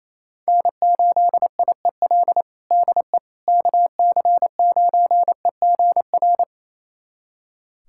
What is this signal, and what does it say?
Morse code, heard as a keyed tone, about 28 wpm, 700 Hz. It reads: N8IEL DE KC9EGR
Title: USA Callsign DE Callsign 28wpm